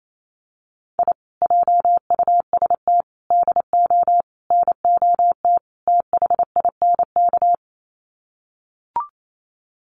Morse code, heard as a keyed tone, about 28 wpm, 700 Hz. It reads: I JUST DO NOT THINK
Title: Sets of 5 Words Encore 28wpm